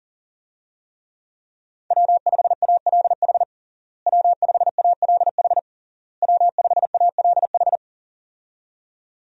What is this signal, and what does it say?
Morse code, heard as a keyed tone, about 40 wpm, 700 Hz. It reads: W5ALH W5ALH W5ALH
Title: USA Call Signs - 5 Characters - Repeated 3x - 40wpm